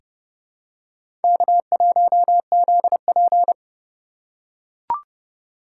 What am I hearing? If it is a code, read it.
Morse code, heard as a keyed tone, about 30 wpm, 700 Hz. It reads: K1ZP
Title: USA Call Signs - 4 Characters - 3x Character Spacing - 30wpm